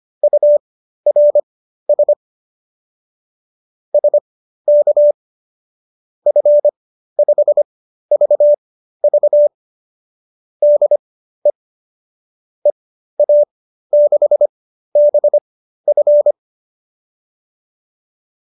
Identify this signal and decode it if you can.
Morse code, heard as a keyed tone, about 25 wpm, spaced as for 13 wpm, 590 Hz. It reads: URS SK F5VV DE EA6BF
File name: qso214 13